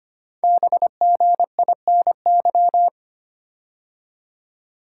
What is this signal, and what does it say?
Morse code, heard as a keyed tone, about 25 wpm, 700 Hz. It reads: BGINY
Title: Code Groups - 5 Letters - 25wpm